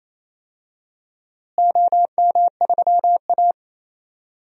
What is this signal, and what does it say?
Morse code, heard as a keyed tone, about 28 wpm, 700 Hz. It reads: OM3A
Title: DXCC-100 Call Signs - 4 Characters - Repeated 3x - 28wpm